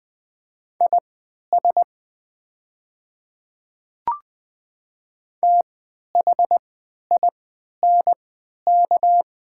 Morse code, heard as a keyed tone, about 20 wpm, spaced as for 11 wpm, 700 Hz. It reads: IS THINK
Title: 3x Character Spacing - Top 200 Words - 20wpm